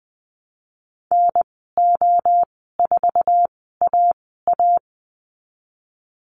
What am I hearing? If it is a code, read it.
Morse code, heard as a keyed tone, about 19 wpm, spaced as for 14 wpm, 700 Hz. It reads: NO4AA
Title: 076 Lesson 13g - Letter L - Call Signs 20wpm